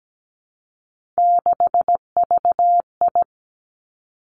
Morse code, heard as a keyed tone, about 17 wpm, 700 Hz. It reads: 6VI
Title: Three Challenging Letter-Number Combinations 17wpm